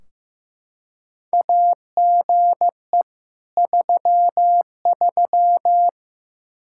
Morse code, heard as a keyed tone, about 15 wpm, 700 Hz. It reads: AGE 33
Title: Practice QSO - Exchange 3 - More Info - 15wpm